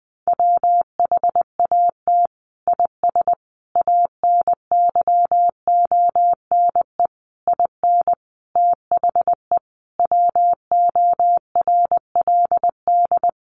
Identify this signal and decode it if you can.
Morse code, heard as a keyed tone, about 20 wpm, 700 Hz. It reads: WHAT IS ANYONE IN THE WORLD
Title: Sentences from Top 900 Words 20wpm